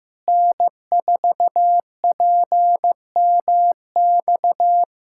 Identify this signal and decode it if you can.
Morse code, heard as a keyed tone, about 15 wpm, 700 Hz. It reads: N4PMX
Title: USA Callsign DE Callsign - 5 Characters - 15wpm